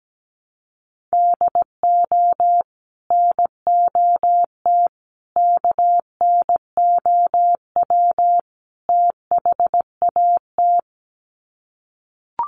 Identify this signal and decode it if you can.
Morse code, heard as a keyed tone, about 17 wpm, 700 Hz. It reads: DO NOT KNOW THAT
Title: Sets of 4 Words 17wpm